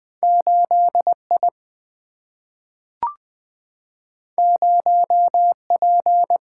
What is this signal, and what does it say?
Morse code, heard as a keyed tone, about 20 wpm, 700 Hz. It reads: 8I 0P
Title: Single-Letter Single-Number 20wpm